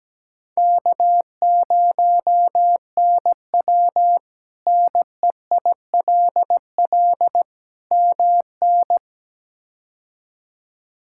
Morse code, heard as a keyed tone, about 17 wpm, 700 Hz. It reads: K0NW NEILL MN
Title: K1USN SST QSO Practice - 17wpm